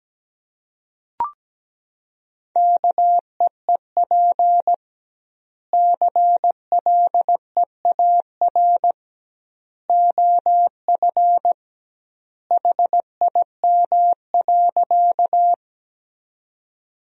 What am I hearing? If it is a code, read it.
Morse code, heard as a keyed tone, about 17 wpm, 700 Hz. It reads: KEEP CLEAR OF HIM.
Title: 2x Word-Spacing - Sentences from Top 500 Words - 17wpm